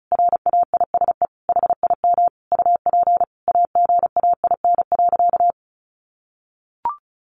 Morse code, heard as a keyed tone, about 34 wpm, 700 Hz. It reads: RAISE HIM UP AGAIN.
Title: Sentences from Top 1400 Words 35wpm